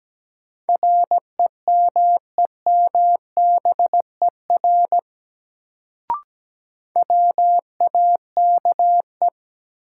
Morse code, heard as a keyed tone, about 17 wpm, 700 Hz. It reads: REMEMBER WAKE
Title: Sets of 4 Words - New Word Drill - 17wpm